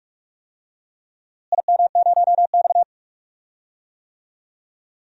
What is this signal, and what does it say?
Morse code, heard as a keyed tone, about 43 wpm, 700 Hz. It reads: IM0X
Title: DXCC-100 Call Signs - 4 Characters - Repeated 3x - 45wpm